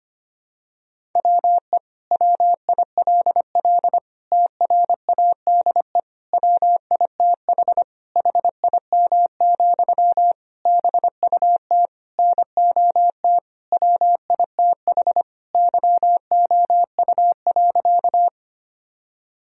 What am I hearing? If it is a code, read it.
Morse code, heard as a keyed tone, about 25 wpm, 700 Hz. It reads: WE WILL TRADE WITH HIM, BUT NOT WITH YOU.
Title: Sentences from Top 1000 Words - Review Part 4 - 25wpm